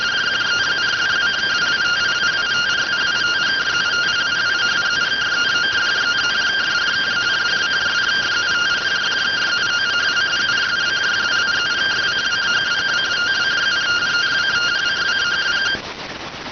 Two-tone teleprinter signal, about 170 Hz shift.